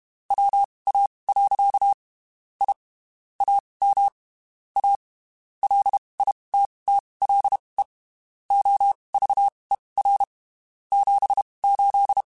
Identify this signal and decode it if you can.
Morse code, heard as a keyed tone, about 32 wpm, spaced as for 23 wpm, 800 Hz. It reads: WA. I AM A LITTLE OVER 78
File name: (spaced) / QSOs Spaced-mp3s / kc7haa @30 sp